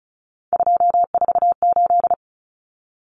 Morse code, heard as a keyed tone, about 34 wpm, 700 Hz. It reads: 248
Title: Numbers - 3-Digit 35wpm